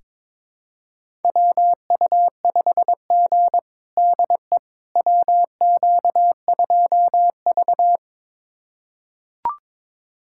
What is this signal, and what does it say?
Morse code, heard as a keyed tone, about 21 wpm, 700 Hz. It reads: WU5G DE WQ2V
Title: USA Callsign DE Callsign - 4 Characters - 22wpm